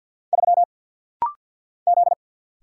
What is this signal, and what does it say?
Morse code, heard as a keyed tone, about 50 wpm, 700 Hz. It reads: 3 Z
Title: Single Letter-Number Alt - Rapid-Fire 50wpm